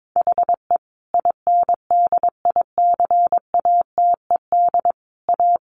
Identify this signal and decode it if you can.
Morse code, heard as a keyed tone, about 22 wpm, 700 Hz. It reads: HE INDICATED A
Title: A Sound Of Thunder 22wpm